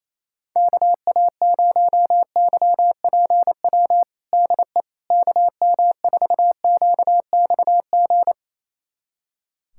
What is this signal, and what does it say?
Morse code, heard as a keyed tone, about 28 wpm, 700 Hz. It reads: KA0YPW DE KM4QXG
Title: USA Callsign DE Callsign - 6 Characters - 28wpm